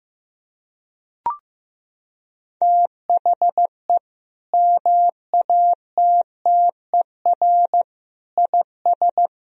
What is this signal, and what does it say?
Morse code, heard as a keyed tone, about 16 wpm, 700 Hz. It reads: THE MATTER IS
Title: Sentences from Top 1000 Words - Review Part 4 - 15wpm